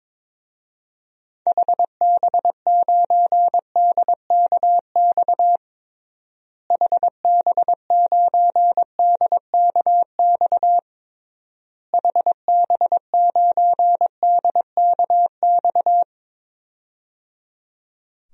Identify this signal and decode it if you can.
Morse code, heard as a keyed tone, about 22 wpm, 700 Hz. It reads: HB9DKX HB9DKX HB9DKX
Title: DXCC-100 Call Signs - 6 Characters - Repeated 3x - 22wpm